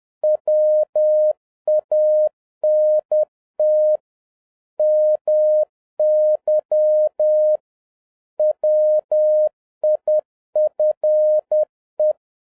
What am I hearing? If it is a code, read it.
Morse code, heard as a keyed tone, about 10 wpm, 610 Hz. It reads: WANT MY WIFE